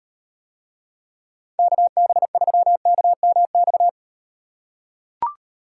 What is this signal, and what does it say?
Morse code, heard as a keyed tone, about 38 wpm, 700 Hz. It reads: KB3KMX